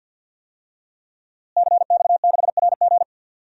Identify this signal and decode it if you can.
Morse code, heard as a keyed tone, about 47 wpm, 700 Hz. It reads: CX6RG
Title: DXCC-100 Call Signs - 5 Characters - Repeated 2x - 50wpm